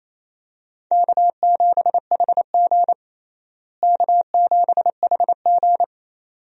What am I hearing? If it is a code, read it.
Morse code, heard as a keyed tone, about 28 wpm, 700 Hz. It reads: K7HG K7HG
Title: USA Call Signs - 4 Characters - Repeated 2x - 28wpm